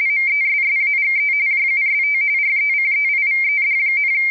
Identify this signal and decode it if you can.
RTTY, about 45.45 baud. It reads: MESSAGE JUST RECEIVED,